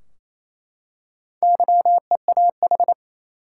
Morse code, heard as a keyed tone, about 28 wpm, 700 Hz. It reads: YEAH
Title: Four Letter Words - 28wpm